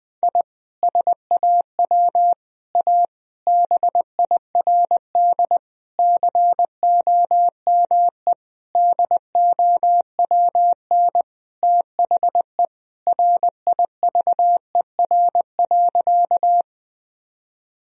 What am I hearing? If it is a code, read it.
Morse code, heard as a keyed tone, about 20 wpm, 700 Hz. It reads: I SAW A BIRD COME DOWN THE RIVER.
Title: Sentences from Top 1000 Words - Review Part 2 - 20wpm